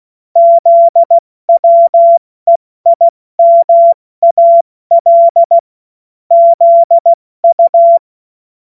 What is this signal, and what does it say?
Morse code, heard as a keyed tone, about 16 wpm, spaced as for 14 wpm, 680 Hz. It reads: ZWEIMAL ZU
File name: Witz-07-Oktober-2025-70-F80